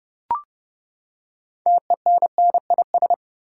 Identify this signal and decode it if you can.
Morse code, heard as a keyed tone, about 30 wpm, 700 Hz. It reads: TENNIS
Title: Six Letter Words - 30wpm